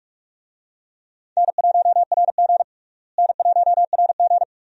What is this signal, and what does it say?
Morse code, heard as a keyed tone, about 46 wpm, 700 Hz. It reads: N1RG N1RG
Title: USA Call Signs - 4 Characters - Repeated 2x - 45wpm